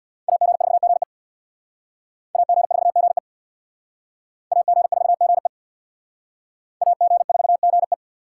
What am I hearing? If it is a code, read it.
Morse code, heard as a keyed tone, about 62 wpm, 690 Hz. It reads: AG4GE AG4GE AG4GE AG4GE
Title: Speed Racing - USA 2x2 Call Signs 50wpm